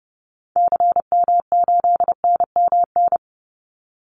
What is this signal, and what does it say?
Morse code, heard as a keyed tone, about 30 wpm, 700 Hz. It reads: CM8NMN
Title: Call Signs Repeated 2x - 30wpm